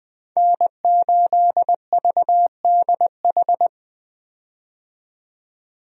Morse code, heard as a keyed tone, about 20 wpm, 700 Hz. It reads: N8VDH
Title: USA Call Signs - 5 Characters - 20wpm